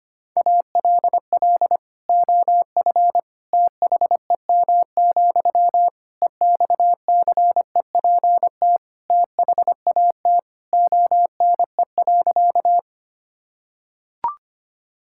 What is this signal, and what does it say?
Morse code, heard as a keyed tone, about 25 wpm, 700 Hz. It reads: ALL OF THEM, EXCEPT THAT ONE.
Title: Sentences from Top 1600 Words 25wpm